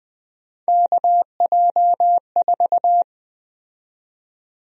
Morse code, heard as a keyed tone, about 20 wpm, 700 Hz. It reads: KJ4
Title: USA Call Sign Prefixes - 20wpm